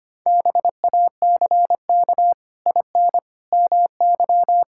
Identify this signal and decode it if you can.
Morse code, heard as a keyed tone, about 25 wpm, 700 Hz. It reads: BACK IN MY
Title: Pangram Challenge v2 - 25wpm